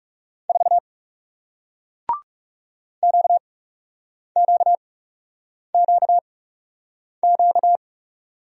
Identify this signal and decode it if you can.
Morse code, heard as a keyed tone, about 37 wpm, spaced as for 8 wpm, 700 Hz. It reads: 4 QQQQ
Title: Speed-Racing Single Letter-Number 30wpm